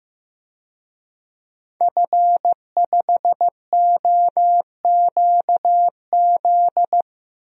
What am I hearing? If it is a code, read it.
Morse code, heard as a keyed tone, about 15 wpm, 700 Hz. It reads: F5OQZ
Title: DXCC-100 Call Signs - 5 Characters - Repeated 2x - 15wpm